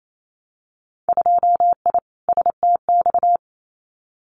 Morse code, heard as a keyed tone, about 28 wpm, 700 Hz. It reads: 2I STX
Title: Winter Field Day Exchanges 28wpm